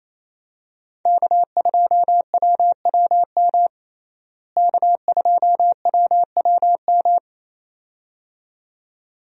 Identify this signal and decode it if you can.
Morse code, heard as a keyed tone, about 27 wpm, 700 Hz. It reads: K2WWM K2WWM
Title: USA Call Signs - 5 Characters - Repeated 2x - 28wpm